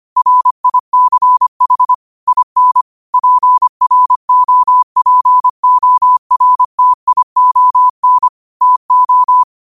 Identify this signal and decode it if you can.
Morse code, heard as a keyed tone, about 25 wpm, 1000 Hz. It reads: RICH IN PROPORTION TO
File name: qotdfiles / 25wpm